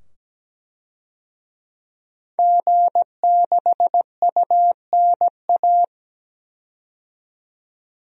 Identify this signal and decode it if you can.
Morse code, heard as a keyed tone, about 17 wpm, 700 Hz. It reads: G6UNA
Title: DXCC-100 Call Signs - 5 Characters - 17wpm